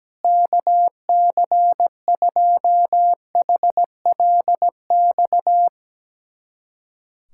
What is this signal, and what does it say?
Morse code, heard as a keyed tone, about 17 wpm, 700 Hz. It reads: KC2HLX
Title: USA Callsign DE Callsign 17wpm